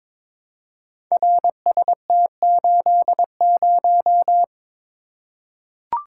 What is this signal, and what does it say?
Morse code, heard as a keyed tone, about 22 wpm, 700 Hz. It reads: RST80
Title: Code Groups - 5 Letter:Numbers - 22wpm